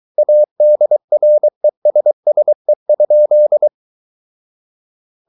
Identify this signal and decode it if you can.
Morse code, heard as a keyed tone, about 23 wpm, 600 Hz. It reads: ADRESSE?
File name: Ferie QRS 23wpm